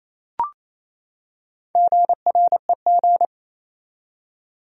Morse code, heard as a keyed tone, about 28 wpm, 700 Hz. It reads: GREG
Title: USA Common Names - Max 5 Letters - 28wpm